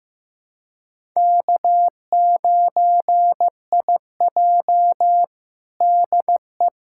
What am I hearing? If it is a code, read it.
Morse code, heard as a keyed tone, about 15 wpm, 700 Hz. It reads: K9IJ DE
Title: USA Callsign DE Callsign - 4 Characters - 15wpm